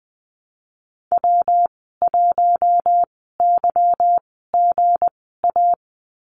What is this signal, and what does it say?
Morse code, heard as a keyed tone, about 20 wpm, spaced as for 15 wpm, 700 Hz. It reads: W1YGA
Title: 167 Lesson 27g - Number 7 - Call Signs 20wpm